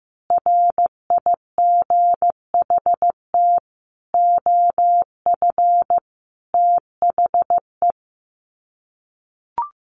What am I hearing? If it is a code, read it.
Morse code, heard as a keyed tone, about 15 wpm, 700 Hz. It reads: RIGHT OF THE
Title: Sets of 5 Words Encore 15wpm